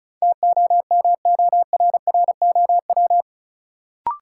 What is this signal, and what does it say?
Morse code, heard as a keyed tone, about 34 wpm, 700 Hz. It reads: TOMORROW
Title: Sentences from Top 1100 Words - New Word Drill Alt - 35wpm